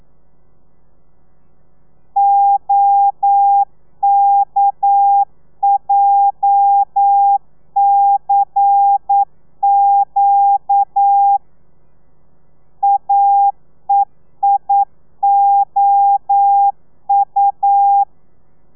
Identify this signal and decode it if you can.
Morse code, heard as a keyed tone, about 9 wpm, 800 Hz. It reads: OKJCQ AEIOU
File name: telegrafia / i2tfj / Lez11 40